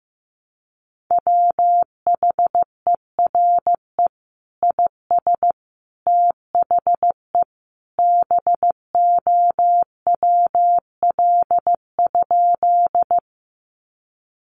Morse code, heard as a keyed tone, about 15 wpm, 700 Hz. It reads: WHERE IS THE BOWL?
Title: Sentences from Top 1200 Words 15wpm